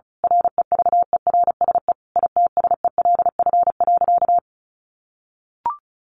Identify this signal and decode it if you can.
Morse code, heard as a keyed tone, about 34 wpm, 700 Hz. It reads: REVERSE ITSELF.
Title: A Sound Of Thunder 35wpm